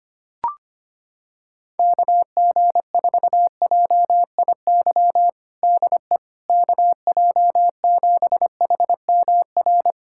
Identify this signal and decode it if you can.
Morse code, heard as a keyed tone, about 25 wpm, 700 Hz. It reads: KG4JIY DE KJ7HMR
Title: USA Callsign DE Callsign 25wpm